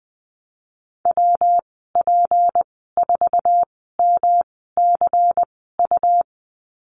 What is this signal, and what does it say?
Morse code, heard as a keyed tone, about 20 wpm, 700 Hz. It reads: W P 4 M C U
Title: 231 Lesson 37g - Letter x - Call Signs 20wpm